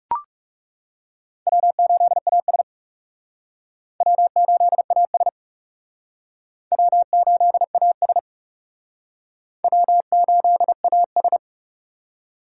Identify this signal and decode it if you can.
Morse code, heard as a keyed tone, about 38 wpm, 700 Hz. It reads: W8AS W8AS W8AS W8AS
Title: Speed Racing - USA 1x2 Call Signs 30wpm